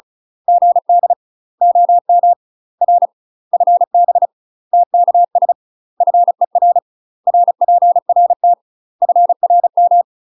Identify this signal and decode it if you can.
Morse code, heard as a keyed tone, about 34 wpm, 700 Hz. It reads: GD OM R FB TKS FER RPRT FRM
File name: UK MASTER QSOs TRAINING / 35WPM / UK MASTER QSOs 0014